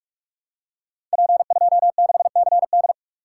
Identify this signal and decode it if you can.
Morse code, heard as a keyed tone, about 44 wpm, 700 Hz. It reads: P26CD
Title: Code Groups - 5 Letter:Numbers - 45wpm